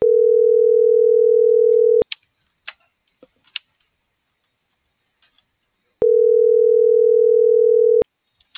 Ringback tone (North America), 440 + 480 Hz.
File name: CENTRAL MORENA / out ram.lmorena.2025